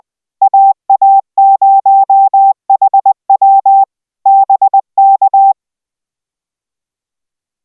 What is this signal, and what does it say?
Morse code, heard as a keyed tone, about 19 wpm, 780 Hz. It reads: AA0HW BK